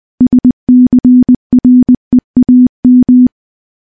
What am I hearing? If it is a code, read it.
Morse code, heard as a keyed tone, about 20 wpm, 250 Hz. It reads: SCREAM
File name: morse / morse-training-20WPM